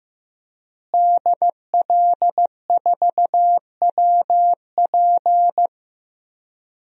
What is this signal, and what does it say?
Morse code, heard as a keyed tone, about 15 wpm, 700 Hz. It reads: DL4WP
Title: DXCC-100 Call Signs - 5 Characters - Repeated 2x - 15wpm